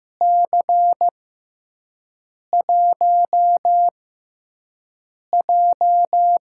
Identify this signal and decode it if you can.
Morse code, heard as a keyed tone, about 15 wpm, spaced as for 5 wpm, 700 Hz. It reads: C1J
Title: USA Call Signs - 6 Characters - 6x Character Spacing - 15wpm